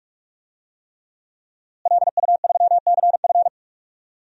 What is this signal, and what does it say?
Morse code, heard as a keyed tone, about 45 wpm, 700 Hz. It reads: LU3CF
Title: DXCC-100 Call Signs - 5 Characters - Repeated 3x - 45wpm